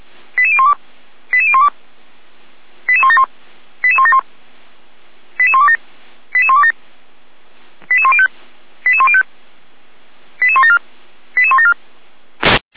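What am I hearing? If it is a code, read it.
Five-tone selective call (ZVEI), heard as quick runs of five tones, tones of about 70 ms. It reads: 88012 88012 88171 88171 88127 88127 88105 88105 88174 88174
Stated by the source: Five-tone ZVEI selective call 88427, received by SDR, from the Feuerwehr Orlamünde audio server.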